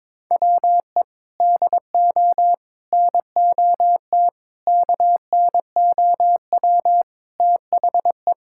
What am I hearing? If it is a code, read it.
Morse code, heard as a keyed tone, about 22 wpm, 700 Hz. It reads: WE DO NOT KNOW THE
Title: Sentences from Top 1000 Words - Review Part 6 - 22wpm